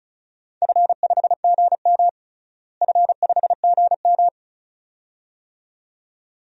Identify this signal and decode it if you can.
Morse code, heard as a keyed tone, about 36 wpm, 700 Hz. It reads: F5GM F5GM
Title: DXCC-100 Call Signs - 4 Characters - Repeated 2x - 35wpm